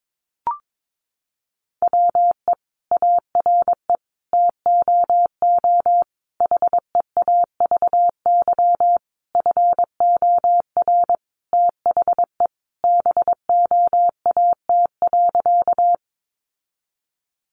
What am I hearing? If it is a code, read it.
Morse code, heard as a keyed tone, about 22 wpm, 700 Hz. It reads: WE ARE TOO HEAVY FOR THE BOAT.
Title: Sentences from Top 1100 Words 22wpm